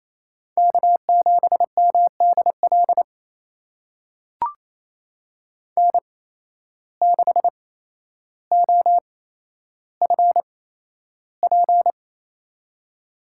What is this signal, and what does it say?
Morse code, heard as a keyed tone, about 29 wpm, spaced as for 12 wpm, 700 Hz. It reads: K7MDL N6OFP
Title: USA Call Signs - 8x Character Spacing - 28wpm